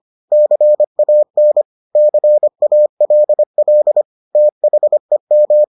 Morse code, heard as a keyed tone, about 25 wpm, 600 Hz. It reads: CAN CALL THEM